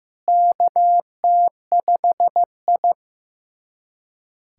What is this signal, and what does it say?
Morse code, heard as a keyed tone, about 15 wpm, 700 Hz. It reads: KT5I
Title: USA Callsign DE Callsign - 4 Characters - 15wpm